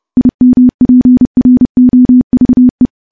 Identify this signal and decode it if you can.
Morse code, heard as a keyed tone, about 30 wpm, 250 Hz. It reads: IMPROVE
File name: morse-training-30WPM